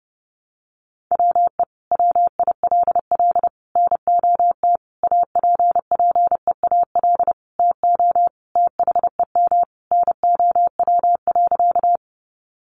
Morse code, heard as a keyed tone, about 30 wpm, 700 Hz. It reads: WE WILL NOT APPEAL TO THEM NOW.
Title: Sentences from Top 1300 Words 30wpm